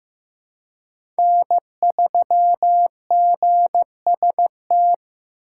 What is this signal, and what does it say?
Morse code, heard as a keyed tone, about 15 wpm, 700 Hz. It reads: N3GST
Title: USA Call Signs - 5 Characters - 3x Character Spacing - 15wpm